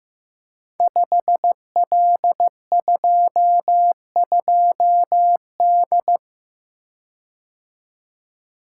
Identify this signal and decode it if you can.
Morse code, heard as a keyed tone, about 15 wpm, 700 Hz. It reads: 5L22D
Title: Code Groups - 5 Letter:Numbers - 15wpm